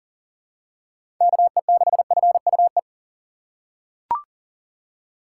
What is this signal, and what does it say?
Morse code, heard as a keyed tone, about 40 wpm, 700 Hz. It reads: KE6FUE
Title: USA Call Signs - 6 Characters - 4x Character Spacing - 40wpm